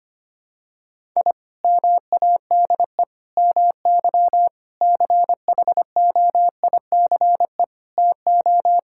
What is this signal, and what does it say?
Morse code, heard as a keyed tone, about 25 wpm, 700 Hz. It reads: I MADE MY CHOICE TO
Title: Sentences from Top 1000 Words - Review Part 6 - 25wpm